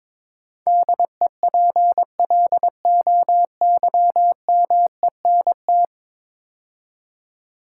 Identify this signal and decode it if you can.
Morse code, heard as a keyed tone, about 22 wpm, 700 Hz. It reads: DEPLOYMENT
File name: practice / 22wpm / 10 Letter Words - First 3 Letters Given - 22wpm